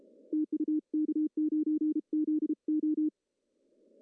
Morse code, heard as a keyed tone, about 33 wpm, 310 Hz. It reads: TU K9ZO
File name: CW / cqww-cw-k5tr